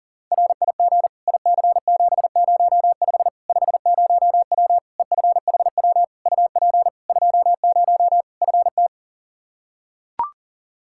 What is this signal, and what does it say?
Morse code, heard as a keyed tone, about 37 wpm, 700 Hz. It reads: RIG IC705 50W EFHW UP 20 FT
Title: Practice QSO - Exchange 2 - Info - 40wpm